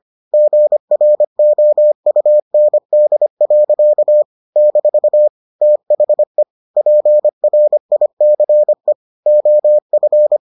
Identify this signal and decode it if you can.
Morse code, heard as a keyed tone, about 25 wpm, 600 Hz. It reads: GROUND. - THE PRICE OF